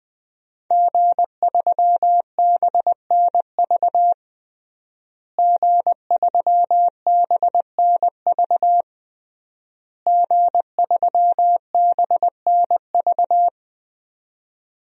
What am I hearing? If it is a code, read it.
Morse code, heard as a keyed tone, about 20 wpm, 700 Hz. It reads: G3BNV G3BNV G3BNV
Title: DXCC-100 Call Signs - 5 Characters - Repeated 3x - 20wpm